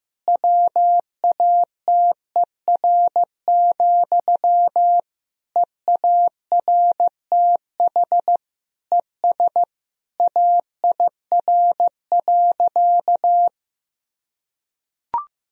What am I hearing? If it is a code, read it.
Morse code, heard as a keyed tone, about 15 wpm, 700 Hz. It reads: WATER, EARTH ES AIR.
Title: Sentences from Top 400 Words and Abbreviations 15wpm